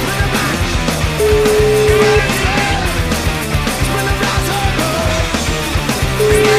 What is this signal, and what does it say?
Ringback tone (Europe, 425 Hz), with noise.